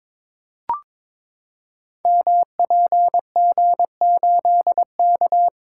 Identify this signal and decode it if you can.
Morse code, heard as a keyed tone, about 23 wpm, 690 Hz. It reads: MPG8K
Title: Code Groups - 5 Letter:Numbers - 22wpm